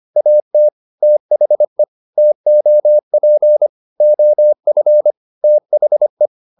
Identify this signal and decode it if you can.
Morse code, heard as a keyed tone, about 25 wpm, 600 Hz. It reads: AT THE TOP OF THE